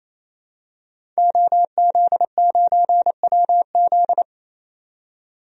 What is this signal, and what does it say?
Morse code, heard as a keyed tone, about 27 wpm, 700 Hz. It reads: OZ9WZ
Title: DXCC-100 Call Signs - 5 Characters - 28wpm